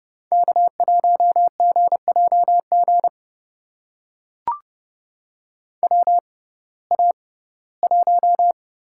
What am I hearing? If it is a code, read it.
Morse code, heard as a keyed tone, about 31 wpm, spaced as for 19 wpm, 700 Hz. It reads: K1GJG WA1
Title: USA Call Signs - 6x Character Spacing - 30wpm